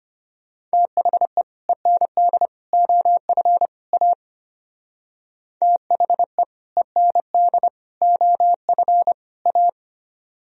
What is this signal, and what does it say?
Morse code, heard as a keyed tone, about 32 wpm, 700 Hz. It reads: THE END OF A THE END OF A